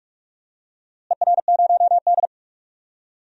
Morse code, heard as a keyed tone, about 46 wpm, 700 Hz. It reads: ER0D
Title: DXCC-100 Call Signs - 4 Characters - Repeated 3x - 45wpm